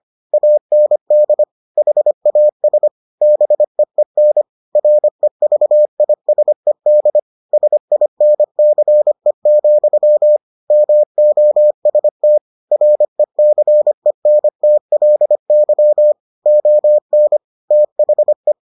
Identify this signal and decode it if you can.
Morse code, heard as a keyed tone, about 25 wpm, 600 Hz. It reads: AND HAS BEEN REVISED SINCE, MOST RECENTLY ON THE